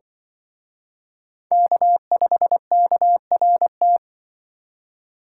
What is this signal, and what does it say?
Morse code, heard as a keyed tone, about 24 wpm, 700 Hz. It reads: K5KRT
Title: Speed Racing - USA 1x3 Call Signs 20wpm